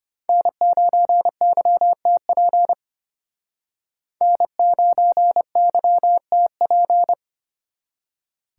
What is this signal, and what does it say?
Morse code, heard as a keyed tone, about 33 wpm, 700 Hz. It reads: N9YTP N9YTP